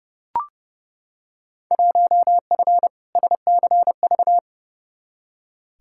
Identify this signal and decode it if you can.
Morse code, heard as a keyed tone, about 30 wpm, 700 Hz. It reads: 1F SCV
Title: Field Day Exchanges Encore - 30wpm